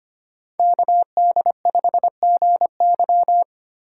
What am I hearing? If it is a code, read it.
Morse code, heard as a keyed tone, about 30 wpm, 700 Hz. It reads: KD5GY
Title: Speed Racing - USA 2x2 Call Signs 25wpm